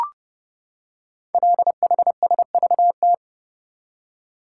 Morse code, heard as a keyed tone, about 30 wpm, 700 Hz. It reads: LHSVT
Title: Code Groups - 5 Letters - 30wpm